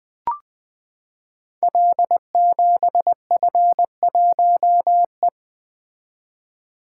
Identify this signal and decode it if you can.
Morse code, heard as a keyed tone, about 20 wpm, 700 Hz. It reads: L7F1E
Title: Code Groups - 5 Letter:Numbers - 20wpm